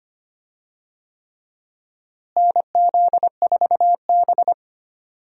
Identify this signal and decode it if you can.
Morse code, heard as a keyed tone, about 25 wpm, 700 Hz. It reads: NZ4B
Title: USA Call Signs - 4 Characters - Repeated 3x - 25wpm